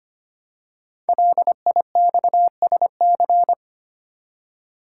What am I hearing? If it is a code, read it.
Morse code, heard as a keyed tone, about 25 wpm, 700 Hz. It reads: LIXSC
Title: Code Groups - 5 Letters - 25wpm